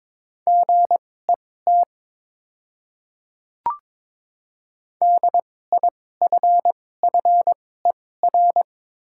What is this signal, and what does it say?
Morse code, heard as a keyed tone, about 22 wpm, spaced as for 16 wpm, 700 Hz. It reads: GET DIFFER
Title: 2x Character Spacing - Top 200 Words - 22wpm